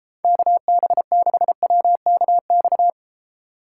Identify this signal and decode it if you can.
Morse code, heard as a keyed tone, about 33 wpm, 700 Hz. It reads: KB6WKX